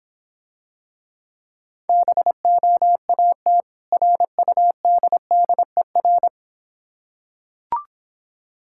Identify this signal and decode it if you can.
Morse code, heard as a keyed tone, about 26 wpm, 700 Hz. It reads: BOAT RUDDER